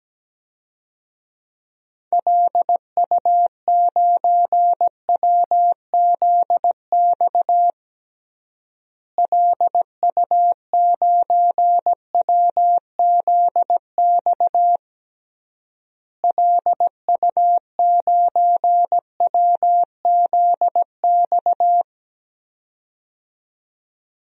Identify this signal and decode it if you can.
Morse code, heard as a keyed tone, about 17 wpm, 700 Hz. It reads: LU9WZX LU9WZX LU9WZX
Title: DXCC-100 Call Signs - 6 Characters - Repeated 3x - 17wpm